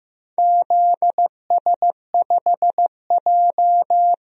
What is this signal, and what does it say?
Morse code, heard as a keyed tone, about 15 wpm, 700 Hz. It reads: ZS5J
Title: DXCC-100 Call Signs - 4 Characters - Repeated 3x - 15wpm